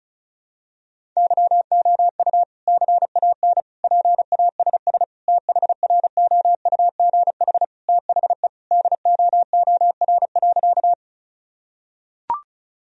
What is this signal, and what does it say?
Morse code, heard as a keyed tone, about 36 wpm, 700 Hz. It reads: YOU CAN PASS THROUGH THE DOOR.
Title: Sentences from Top 1000 Words - Review Part 1 - 35wpm